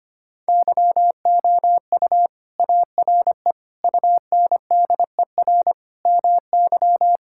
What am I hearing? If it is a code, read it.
Morse code, heard as a keyed tone, about 25 wpm, 700 Hz. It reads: YOU ARE UNDER MY
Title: Sentences from Top 1600 Words 25wpm